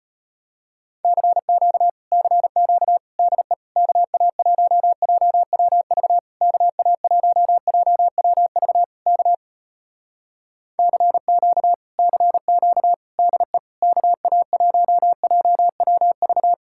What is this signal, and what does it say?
Morse code, heard as a keyed tone, about 33 wpm, 700 Hz. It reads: CQ CQ DE KA1JWV KA1JWV K CQ CQ DE KA1JWV